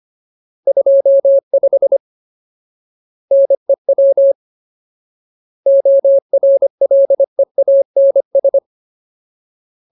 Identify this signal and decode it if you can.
Morse code, heard as a keyed tone, about 25 wpm, 550 Hz. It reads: 25 NEW ORLEANS
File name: Ultimate Word List / 25WPM Ultimate